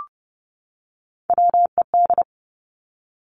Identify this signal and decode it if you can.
Morse code, heard as a keyed tone, about 30 wpm, 700 Hz. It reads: WED
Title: Three Challenging Letter-Number Combinations 30wpm